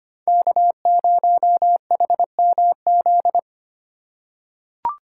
Morse code, heard as a keyed tone, about 25 wpm, 700 Hz. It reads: K0HMZ
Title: K1USN SST QSO Practice - 25wpm